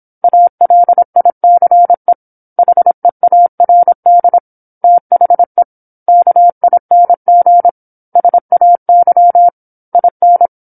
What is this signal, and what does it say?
Morse code, heard as a keyed tone, about 27 wpm, 700 Hz. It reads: ALICE HEARD THE KING SAY IN
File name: 08-Chapter-08 26 WPM